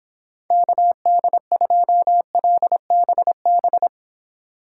Morse code, heard as a keyed tone, about 30 wpm, 700 Hz. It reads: KD2LBB